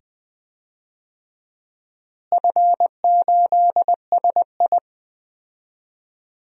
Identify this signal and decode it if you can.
Morse code, heard as a keyed tone, about 20 wpm, 700 Hz. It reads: F8SI
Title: DXCC-100 Call Signs - 4 Characters - Repeated 2x - 20wpm